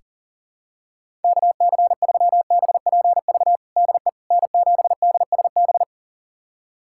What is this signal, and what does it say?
Morse code, heard as a keyed tone, about 40 wpm, 700 Hz. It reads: KC3BPV DE N7DSB
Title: USA Callsign DE Callsign 40wpm